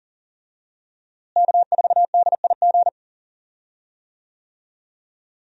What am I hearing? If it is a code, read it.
Morse code, heard as a keyed tone, about 40 wpm, 700 Hz. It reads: K4DIG
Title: USA Call Signs - 5 Characters - Repeated 2x - 40wpm